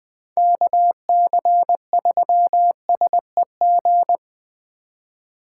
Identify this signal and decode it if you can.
Morse code, heard as a keyed tone, about 20 wpm, 700 Hz. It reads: KC3SEG
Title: 2023 POTA Activator Call Signs - 20wpm